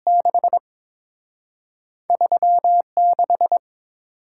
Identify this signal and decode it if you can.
Morse code, heard as a keyed tone, about 30 wpm, spaced as for 26 wpm, 700 Hz. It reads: 6 36
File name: Speed-Racing - Two Letter-Number Combinations - 22wpm